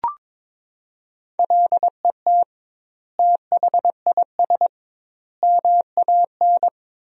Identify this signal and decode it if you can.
Morse code, heard as a keyed tone, about 22 wpm, 700 Hz. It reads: LET THIS MAN
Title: 2x Word-Spacing - Sentences from Top 400 Words - 22wpm